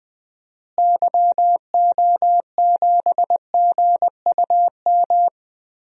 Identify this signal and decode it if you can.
Morse code, heard as a keyed tone, about 20 wpm, 700 Hz. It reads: YO7GUM
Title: DXCC-100 Call Signs - 6 Characters - Repeated 2x - 20wpm